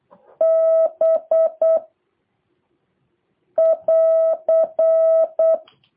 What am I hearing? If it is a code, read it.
Morse code, heard as a keyed tone, about 8 wpm, 650 Hz. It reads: B <AR>
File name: jh8cbh / 2023cw